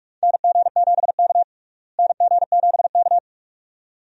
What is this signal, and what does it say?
Morse code, heard as a keyed tone, about 47 wpm, 700 Hz. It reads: NG7K NG7K
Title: USA Call Signs - 4 Characters - Repeated 2x - 45wpm